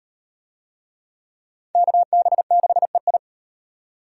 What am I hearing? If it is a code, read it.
Morse code, heard as a keyed tone, about 38 wpm, 700 Hz. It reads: KB6EI